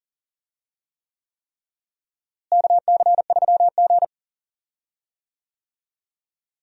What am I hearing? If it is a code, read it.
Morse code, heard as a keyed tone, about 40 wpm, 700 Hz. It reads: KC3G
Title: USA Call Signs - 4 Characters - Repeated 3x - 40wpm